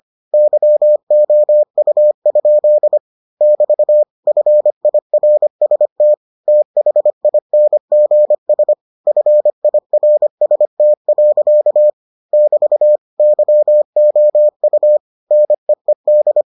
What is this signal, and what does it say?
Morse code, heard as a keyed tone, about 25 wpm, 600 Hz. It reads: YOU? <BT> FIRST THINGS FIRST. <BT> YOU NEED